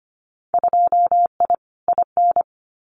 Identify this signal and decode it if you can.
Morse code, heard as a keyed tone, about 25 wpm, 700 Hz. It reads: 2I IN
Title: Winter Field Day Exchanges Encore 25wpm